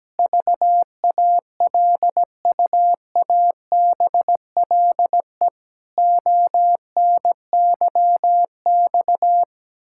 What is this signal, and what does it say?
Morse code, heard as a keyed tone, about 17 wpm, 700 Hz. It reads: VALUABLE ONYX
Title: Pangram Challenge v2 - 17wpm